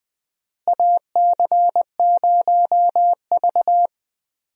Morse code, heard as a keyed tone, about 20 wpm, 700 Hz. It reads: AC0V
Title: USA Call Signs - 4 Characters - Repeated 3x - 20wpm